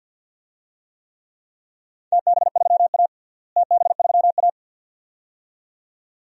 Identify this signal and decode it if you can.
Morse code, heard as a keyed tone, about 48 wpm, 700 Hz. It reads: TB3A TB3A
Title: DXCC-100 Call Signs - 4 Characters - Repeated 2x - 50wpm